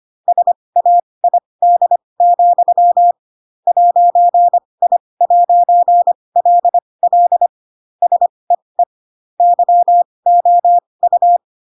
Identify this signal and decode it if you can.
Morse code, heard as a keyed tone, about 25 wpm, spaced as for 20 wpm, 700 Hz. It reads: SAID, 'I'LL SEE YOU
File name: Pinocchio / Pinocchio ch7 15 25 ik1hge